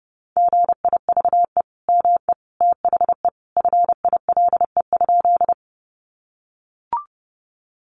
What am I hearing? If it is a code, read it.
Morse code, heard as a keyed tone, about 30 wpm, 700 Hz. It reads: GIVE ME THE FILE?
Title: Sentences from Top 1200 Words 30wpm